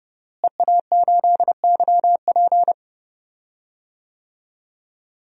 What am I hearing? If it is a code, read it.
Morse code, heard as a keyed tone, about 31 wpm, 700 Hz. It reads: EA8YP
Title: DXCC-100 Call Signs - 5 Characters - Repeated 3x - 30wpm